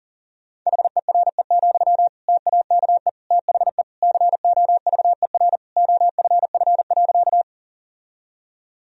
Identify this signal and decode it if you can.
Morse code, heard as a keyed tone, about 40 wpm, 700 Hz. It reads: HERE, TAKE THE COVER OFF.
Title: Sentences from Top 1000 Words - Review Part 1 - 40wpm